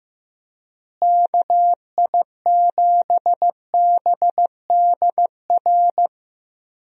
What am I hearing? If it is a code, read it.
Morse code, heard as a keyed tone, about 15 wpm, 700 Hz. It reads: KI7BDR
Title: USA Call Signs - 6 Characters - Repeated 3x - 15wpm